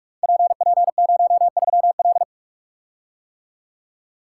Morse code, heard as a keyed tone, about 44 wpm, 700 Hz. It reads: PP03L
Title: Code Groups - 5 Letter:Numbers - 45wpm